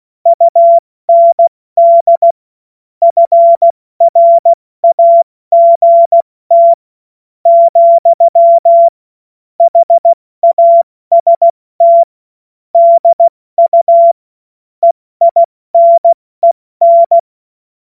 Morse code, heard as a keyed tone, about 16 wpm, spaced as for 13 wpm, 680 Hz. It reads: UND FRAGT, HAST DU EINEN
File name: Morsekurs / Witze / Witz 24. August 2021, 70 F80